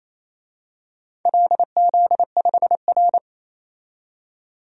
Morse code, heard as a keyed tone, about 27 wpm, 700 Hz. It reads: LZ5R
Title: DXCC-100 Call Signs - 4 Characters - 28wpm